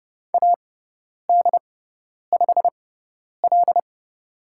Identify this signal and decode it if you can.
Morse code, heard as a keyed tone, about 30 wpm, spaced as for 10 wpm, 700 Hz. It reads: AD5L
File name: Lesson 17g - Letter u - Call Signs 30wpm